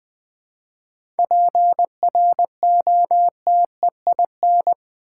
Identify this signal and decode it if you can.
Morse code, heard as a keyed tone, about 20 wpm, 700 Hz. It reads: PROTEIN
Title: 7 Letter Words - First 3 Letters Given - 20wpm